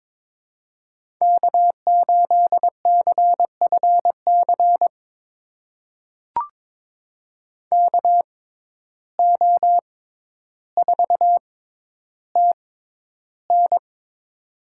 Morse code, heard as a keyed tone, about 22 wpm, spaced as for 11 wpm, 700 Hz. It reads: K8CFC KO4TN
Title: USA Call Signs - 5 Characters - 6x Character Spacing - 22wpm